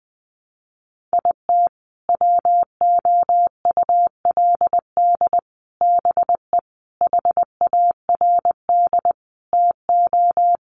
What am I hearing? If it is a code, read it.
Morse code, heard as a keyed tone, about 20 wpm, 700 Hz. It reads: IT WOULD BE HARD TO
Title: Sets of 5 Words Encore2 20wpm